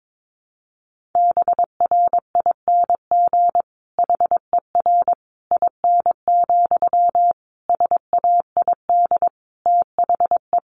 Morse code, heard as a keyed tone, about 22 wpm, 700 Hz. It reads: BRING HER IN, SAID THE
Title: Sentences from Top 1000 Words 22wpm